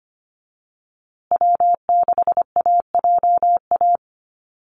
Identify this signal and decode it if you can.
Morse code, heard as a keyed tone, about 25 wpm, 700 Hz. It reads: W6AJA
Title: USA Call Signs Repeated 2x 25wpm